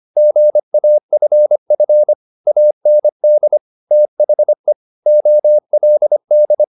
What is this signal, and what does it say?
Morse code, heard as a keyed tone, about 25 wpm, 600 Hz. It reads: GAFF AND THE OLD
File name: the old man and the sea part 1 25wpm